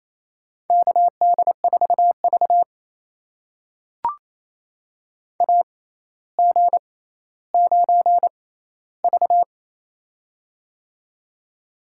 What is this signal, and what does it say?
Morse code, heard as a keyed tone, about 29 wpm, spaced as for 15 wpm, 700 Hz. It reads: KD4V AG9V
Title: USA Call Signs - 4 Characters - 6x Character Spacing - 28wpm